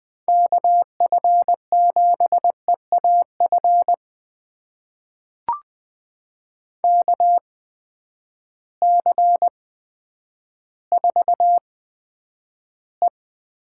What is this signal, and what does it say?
Morse code, heard as a keyed tone, about 20 wpm, spaced as for 10 wpm, 700 Hz. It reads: KF7EAF KC4E
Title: USA Call Signs - 6 Characters - 8x Character Spacing - 20wpm